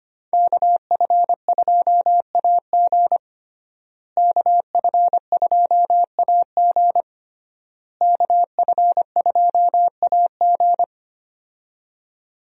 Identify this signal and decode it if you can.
Morse code, heard as a keyed tone, about 25 wpm, 700 Hz. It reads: KF2AG KF2AG KF2AG
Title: USA Call Signs - 5 Characters - Repeated 3x - 25wpm